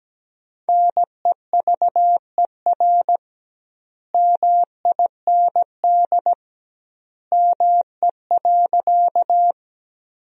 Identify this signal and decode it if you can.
Morse code, heard as a keyed tone, about 17 wpm, 700 Hz. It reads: NEVER MIND ME.
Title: 2x Word-Spacing - Sentences from Top 500 Words - 17wpm